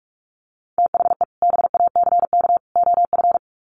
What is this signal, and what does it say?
Morse code, heard as a keyed tone, about 44 wpm, 700 Hz. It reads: THE BACK OF
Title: Sets of 3 Words 45wpm